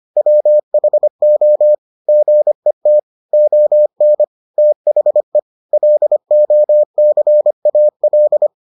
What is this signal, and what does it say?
Morse code, heard as a keyed tone, about 25 wpm, 600 Hz. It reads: WHO GET ON THE LOCAL